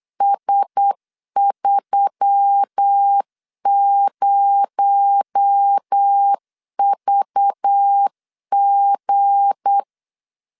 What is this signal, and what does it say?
Morse code, heard as a keyed tone, about 8 wpm, 800 Hz. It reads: S30VG
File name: LEZIONI / 8 YJ45 / 8 Y 8wpm 3245